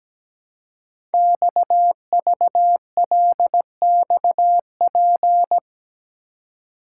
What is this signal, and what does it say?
Morse code, heard as a keyed tone, about 17 wpm, 700 Hz. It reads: XVLXP
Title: Code Groups - 5 Letters - 17wpm